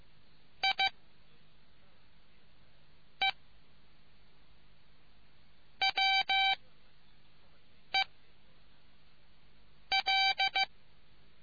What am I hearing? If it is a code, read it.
Morse code, heard as a keyed tone, about 15 wpm, 770 Hz. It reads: I E W E L